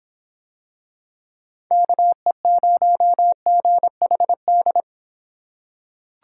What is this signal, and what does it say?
Morse code, heard as a keyed tone, about 25 wpm, 700 Hz. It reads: KE0GHD